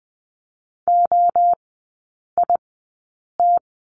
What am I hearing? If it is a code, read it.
Morse code, heard as a keyed tone, about 20 wpm, spaced as for 8 wpm, 700 Hz. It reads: OIT
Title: 035 Lesson 7e - Letter s - Three Characters 20wpm